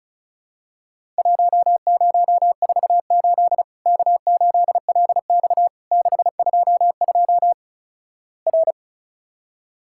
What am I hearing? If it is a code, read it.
Morse code, heard as a keyed tone, about 36 wpm, 700 Hz. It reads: 1048 K8LX 622 R
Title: Worked All Europe DX Contest - QTC Exchange - 35wpm